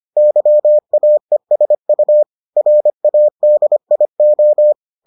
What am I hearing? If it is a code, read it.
Morse code, heard as a keyed tone, about 25 wpm, 600 Hz. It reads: YAESU RADIO